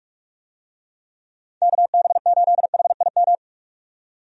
Morse code, heard as a keyed tone, about 41 wpm, 700 Hz. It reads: KB8HIM
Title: USA Call Signs - 6 Characters - 45wpm